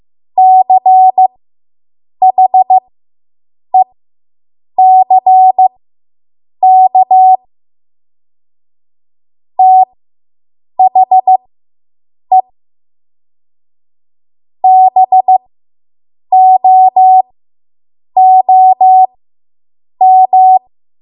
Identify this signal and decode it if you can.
Morse code, heard as a keyed tone, about 15 wpm, spaced as for 7 wpm, 750 Hz. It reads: CHECK THE BOOM